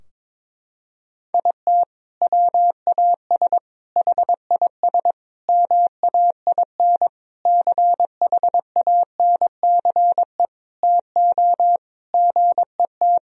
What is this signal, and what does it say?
Morse code, heard as a keyed tone, about 22 wpm, 700 Hz. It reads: IT WAS HIS MAIN CHANCE TO GET
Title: Sentences from Top 1000 Words - Review Part 2 - 22wpm